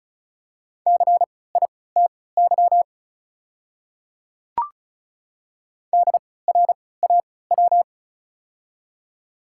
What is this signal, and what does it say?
Morse code, heard as a keyed tone, about 36 wpm, spaced as for 20 wpm, 700 Hz. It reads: CITY DRAW
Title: 3x Character Spacing - Top 500 Words - 35wpm